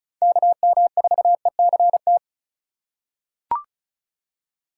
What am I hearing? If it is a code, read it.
Morse code, heard as a keyed tone, about 34 wpm, 700 Hz. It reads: KM4ECT
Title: USA Call Signs - 6 Characters - 4x Character Spacing - 35wpm